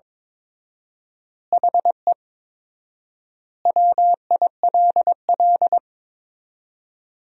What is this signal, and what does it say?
Morse code, heard as a keyed tone, about 22 wpm, 700 Hz. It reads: HE WILL
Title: 4x Word-Spacing - Sentences from Top 300 Words - 22wpm